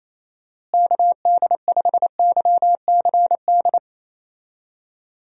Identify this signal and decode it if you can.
Morse code, heard as a keyed tone, about 28 wpm, 700 Hz. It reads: KD5YCD
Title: USA Call Signs - 6 Characters - Repeated 2x - 28wpm